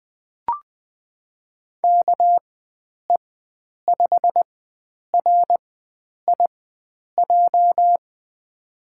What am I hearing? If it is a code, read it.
Morse code, heard as a keyed tone, about 21 wpm, spaced as for 9 wpm, 700 Hz. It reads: KE5RIJ
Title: USA Call Signs - 6 Characters - 4x Character Spacing - 20wpm